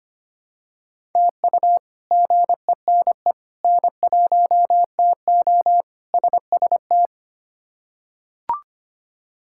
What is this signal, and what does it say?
Morse code, heard as a keyed tone, about 25 wpm, 700 Hz. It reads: TU GENE N1TO SST
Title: K1USN SST QSO Practice - 25wpm